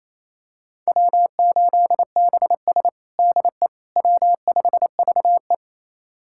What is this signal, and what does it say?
Morse code, heard as a keyed tone, about 28 wpm, 700 Hz. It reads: W8BS DE W5VE
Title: USA Callsign DE Callsign - 4 Characters - 28wpm